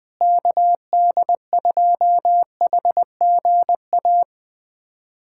Morse code, heard as a keyed tone, about 21 wpm, 700 Hz. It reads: KD2HGA
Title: USA Callsign DE Callsign 20wpm